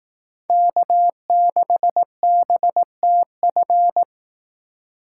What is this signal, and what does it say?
Morse code, heard as a keyed tone, about 19 wpm, 700 Hz. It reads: K6BTF
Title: Speed Racing - USA 1x3 Call Signs 15wpm